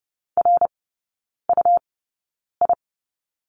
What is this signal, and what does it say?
Morse code, heard as a keyed tone, about 30 wpm, spaced as for 9 wpm, 690 Hz. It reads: RUI
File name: Lesson 17e - Letter u - Three Characters 30wpm